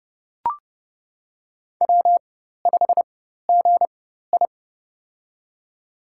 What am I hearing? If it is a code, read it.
Morse code, heard as a keyed tone, about 30 wpm, spaced as for 14 wpm, 700 Hz. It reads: W5GI
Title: USA Call Signs - 4 Characters - 4x Character Spacing - 30wpm